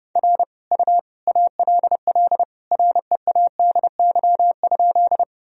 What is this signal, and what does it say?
Morse code, heard as a keyed tone, about 32 wpm, 700 Hz. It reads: R U ALL READY?
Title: Sentences from Top 400 Words and Abbreviations 30wpm